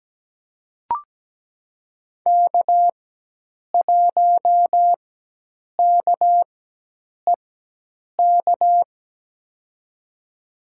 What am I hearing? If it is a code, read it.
Morse code, heard as a keyed tone, about 17 wpm, spaced as for 8 wpm, 700 Hz. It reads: K1KEK
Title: USA Call Signs - 5 Characters - 4x Character Spacing - 17wpm